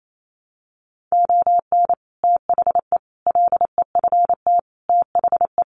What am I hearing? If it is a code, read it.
Morse code, heard as a keyed tone, about 28 wpm, 700 Hz. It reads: ON THE LEFT THE
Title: Sentences from Top 700 Words 28wpm